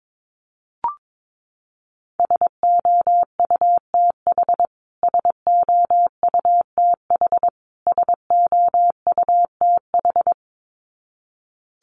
Morse code, heard as a keyed tone, about 22 wpm, 700 Hz. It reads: SOUTH SOUTH SOUTH
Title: Top 500 Words Repeated 22wpm